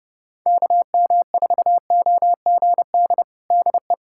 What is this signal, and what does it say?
Morse code, heard as a keyed tone, about 29 wpm, 700 Hz. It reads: KM4OGD DE
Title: USA Callsign DE Callsign - 6 Characters - 30wpm